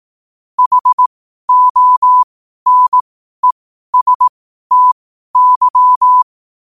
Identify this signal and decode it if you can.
Morse code, heard as a keyed tone, about 18 wpm, spaced as for 12 wpm, 1000 Hz. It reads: HONESTY
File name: qotdfiles / 12wpm